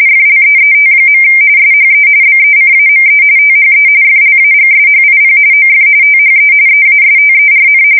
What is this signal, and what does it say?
RTTY, about 45.45 baud. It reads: FLEET. SHE BECAME THE FLAGSHIP OF THE EASTERN
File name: fsk170 / USS-New-Jersey